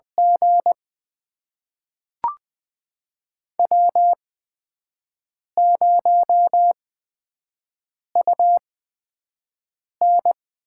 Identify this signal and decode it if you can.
Morse code, heard as a keyed tone, about 20 wpm, spaced as for 5 wpm, 700 Hz. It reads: G W0UN
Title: USA Call Signs - 4 Characters - 8x Character Spacing - 20wpm